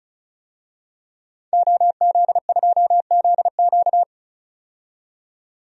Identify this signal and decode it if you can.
Morse code, heard as a keyed tone, about 34 wpm, 700 Hz. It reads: OZ2ZQ
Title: DXCC-100 Call Signs - 5 Characters - 35wpm